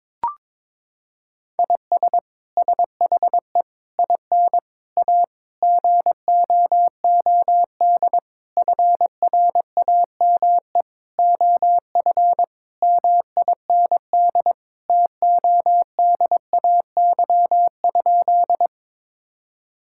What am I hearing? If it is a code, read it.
Morse code, heard as a keyed tone, about 22 wpm, 700 Hz. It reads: IS SHE IN A GOOD FRAME OF MIND TODAY?
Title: Sentences from Top 1000 Words - Review Part 6 - 22wpm